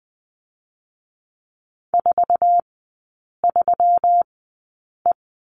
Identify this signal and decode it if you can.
Morse code, heard as a keyed tone, about 20 wpm, spaced as for 8 wpm, 690 Hz. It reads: 43E
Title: 133 Lesson 22e - Question Mark - Three Characters 20wpm